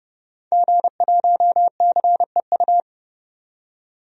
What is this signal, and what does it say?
Morse code, heard as a keyed tone, about 31 wpm, 700 Hz. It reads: G1CEU
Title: DXCC-100 Call Signs - 5 Characters - 30wpm